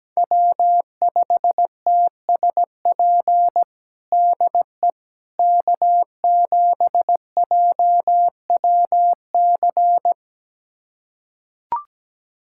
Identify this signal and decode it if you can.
Morse code, heard as a keyed tone, about 17 wpm, 700 Hz. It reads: W5TSP DE K7JWC
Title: USA Callsign DE Callsign - 5 Characters - 17wpm